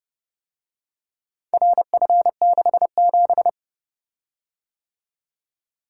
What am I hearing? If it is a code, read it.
Morse code, heard as a keyed tone, about 29 wpm, 700 Hz. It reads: RF67